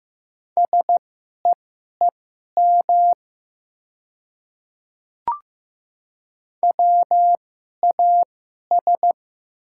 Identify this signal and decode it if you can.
Morse code, heard as a keyed tone, about 15 wpm, spaced as for 11 wpm, 700 Hz. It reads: SEEM WAS
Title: 2x Character Spacing - Top 300 Words - 15wpm